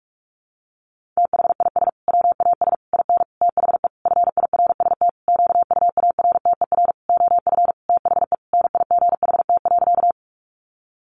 Morse code, heard as a keyed tone, about 44 wpm, 700 Hz. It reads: THIS WAS IN THE FIRST QUARTER OF THE NIGHT.
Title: Sentences from Top 1300 Words 45wpm